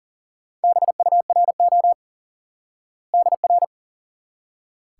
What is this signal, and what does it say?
Morse code, heard as a keyed tone, about 40 wpm, 700 Hz. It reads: BURO DR
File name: IQE Course 3 - Lesson 89 - YAGI - Two Elements - 40wpm